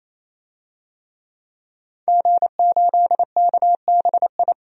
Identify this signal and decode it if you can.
Morse code, heard as a keyed tone, about 28 wpm, 700 Hz. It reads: G8KBI
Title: DXCC-100 Call Signs - 5 Characters - 28wpm